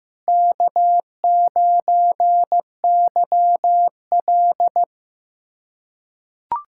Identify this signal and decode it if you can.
Morse code, heard as a keyed tone, about 15 wpm, 700 Hz. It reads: K9YL
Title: USA Callsign DE Callsign - 4 Characters - 15wpm